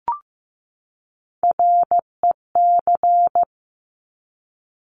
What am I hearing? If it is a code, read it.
Morse code, heard as a keyed tone, about 15 wpm, 700 Hz. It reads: REC
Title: Top Three Letter Combinations in Words 15wpm